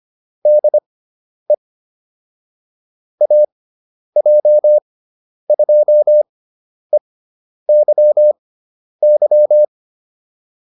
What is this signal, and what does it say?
Morse code, heard as a keyed tone, about 25 wpm, spaced as for 10 wpm, 600 Hz. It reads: DE AJ2EYY